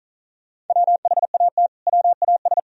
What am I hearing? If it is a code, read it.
Morse code, heard as a keyed tone, about 47 wpm, 700 Hz. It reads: WHAT WAS